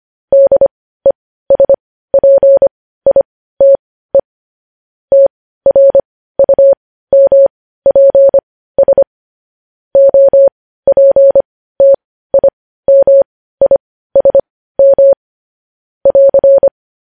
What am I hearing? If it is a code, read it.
Morse code, heard as a keyed tone, about 25 wpm, spaced as for 15 wpm, 550 Hz. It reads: DESPITE TRUMPS OPTIMISM <AR>
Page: news headlines in morse code at 15 wpm